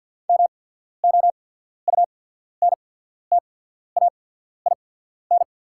Morse code, heard as a keyed tone, about 49 wpm, spaced as for 14 wpm, 690 Hz. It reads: MOUNTAIN
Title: 8x Character Spacing - Top 300 Words - 50wpm